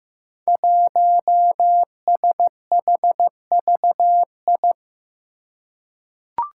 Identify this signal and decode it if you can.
Morse code, heard as a keyed tone, about 15 wpm, 700 Hz. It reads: 1SHVI
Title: Five Challenging Letter-Number Combinations 15wpm